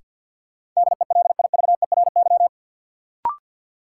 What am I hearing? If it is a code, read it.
Morse code, heard as a keyed tone, about 48 wpm, 700 Hz. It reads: DELIVERY
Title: Sentences from Top 1500 Words - New Word Drill Alt - 50wpm